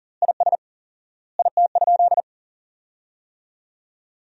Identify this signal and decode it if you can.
Morse code, heard as a keyed tone, about 40 wpm, 700 Hz. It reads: IS IT?
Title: 4x Word-Spacing - Sentences from Top 100 Words - 40wpm